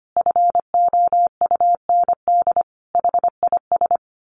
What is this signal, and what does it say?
Morse code, heard as a keyed tone, about 25 wpm, 700 Hz. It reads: FOUND HIS
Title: A Sound Of Thunder 25wpm